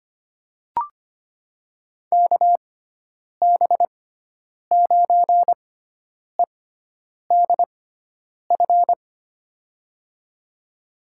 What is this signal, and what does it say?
Morse code, heard as a keyed tone, about 25 wpm, spaced as for 9 wpm, 700 Hz. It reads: KB9EDF
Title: USA Call Signs - 6 Characters - 6x Character Spacing - 25wpm